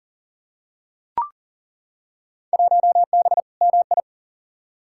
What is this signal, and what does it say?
Morse code, heard as a keyed tone, about 40 wpm, 700 Hz. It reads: 1B MI
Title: Field Day Exchanges - 40wpm